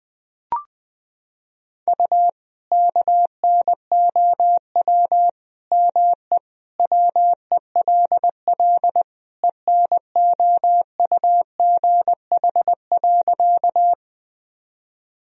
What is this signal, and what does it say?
Morse code, heard as a keyed tone, about 21 wpm, 700 Hz. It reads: U KNOW ME WELL ENOUGH.
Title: Sentences from Top 400 Words and Abbreviations 20wpm